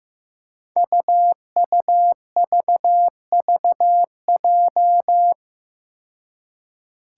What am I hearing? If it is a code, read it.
Morse code, heard as a keyed tone, about 15 wpm, 700 Hz. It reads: UUVVJ
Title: Five Challenging Letter-Number Combinations 15wpm